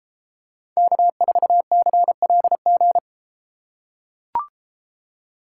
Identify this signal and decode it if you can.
Morse code, heard as a keyed tone, about 31 wpm, 700 Hz. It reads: K4CLG